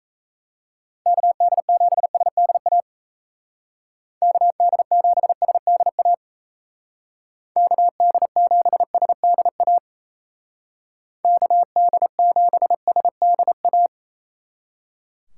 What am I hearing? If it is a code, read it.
Morse code, heard as a keyed tone, about 34 wpm, 700 Hz. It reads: KD7SDA KD7SDA KD7SDA KD7SDA
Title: Speed Racing - USA 2x3 Call Signs 28wpm